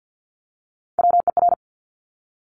Morse code, heard as a keyed tone, about 50 wpm, 700 Hz. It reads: WER
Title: Top Three Letter Combinations in Words 50wpm